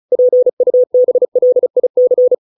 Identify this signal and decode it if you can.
Morse code, heard as a keyed tone, about 34 wpm, 500 Hz. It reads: PUBLIC